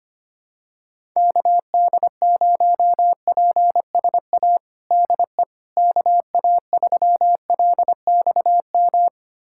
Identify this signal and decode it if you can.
Morse code, heard as a keyed tone, about 25 wpm, 700 Hz. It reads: KD0PSA DE KA3LXM
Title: USA Callsign DE Callsign - 6 Characters - 25wpm